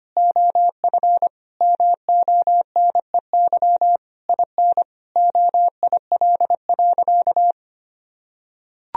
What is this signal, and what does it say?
Morse code, heard as a keyed tone, about 25 wpm, 700 Hz. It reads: OF MONEY IN OIL.
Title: Sentences from Top 1000 Words - Review Part 4 - 25wpm